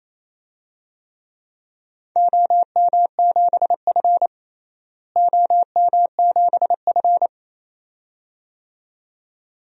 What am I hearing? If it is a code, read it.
Morse code, heard as a keyed tone, about 28 wpm, 700 Hz. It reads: OM7F OM7F
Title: DXCC-100 Call Signs - 4 Characters - Repeated 2x - 28wpm